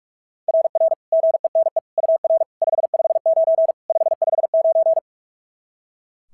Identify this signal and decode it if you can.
Morse code, heard as a keyed tone, about 42 wpm, 690 Hz. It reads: RR GENE UR 559 559
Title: SOTA QSO Practice 45wpm